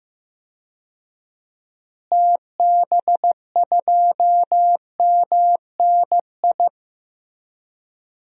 Morse code, heard as a keyed tone, about 15 wpm, 700 Hz. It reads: TB2MNI
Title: DXCC-100 Call Signs - 6 Characters - Repeated 2x - 15wpm